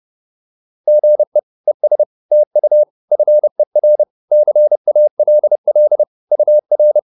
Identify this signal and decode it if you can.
Morse code, heard as a keyed tone, about 30 wpm, 600 Hz. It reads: GE ES TU FER CALL UR